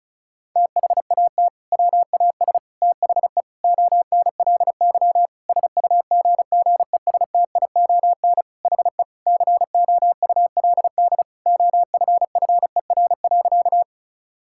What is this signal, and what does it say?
Morse code, heard as a keyed tone, about 36 wpm, 700 Hz. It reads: THAT WAS THE ONLY SUGGESTION HE COULD OFFER.
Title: Sentences from Top 1600 Words 35wpm